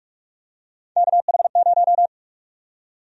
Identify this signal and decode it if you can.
Morse code, heard as a keyed tone, about 46 wpm, 700 Hz. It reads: KH0
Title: USA Call Sign Prefixes - 45wpm